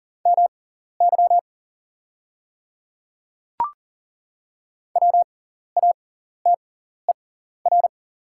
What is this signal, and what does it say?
Morse code, heard as a keyed tone, about 40 wpm, spaced as for 14 wpm, 690 Hz. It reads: MY WATER
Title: 6x Character Spacing - Top 100 Words - 40wpm